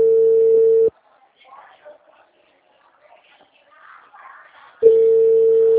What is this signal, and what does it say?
Ringback tone (China, 450 Hz), with speech.